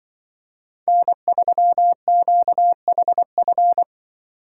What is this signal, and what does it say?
Morse code, heard as a keyed tone, about 24 wpm, 700 Hz. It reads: N3QHF
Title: Speed Racing - USA 1x3 Call Signs 20wpm